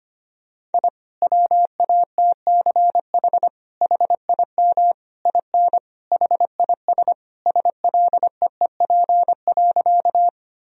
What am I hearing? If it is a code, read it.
Morse code, heard as a keyed tone, about 25 wpm, 700 Hz. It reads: I WATCH HIM IN HIS SLEEP.
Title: Sentences from Top 1000 Words - Review Part 1 - 25wpm